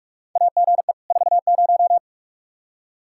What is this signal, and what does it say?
Morse code, heard as a keyed tone, about 42 wpm, 700 Hz. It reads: AGE 40
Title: Practice QSO - Exchange 3 - More Info - 45wpm